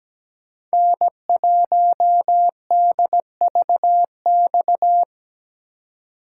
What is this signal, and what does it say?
Morse code, heard as a keyed tone, about 20 wpm, 700 Hz. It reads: N1DVX
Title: Speed Racing - USA 1x3 Call Signs 17wpm